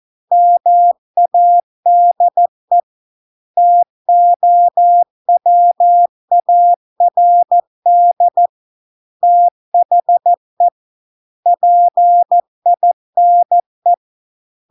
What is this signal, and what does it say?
Morse code, heard as a keyed tone, about 14 wpm, 700 Hz. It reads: MADE TOWARD THE PINE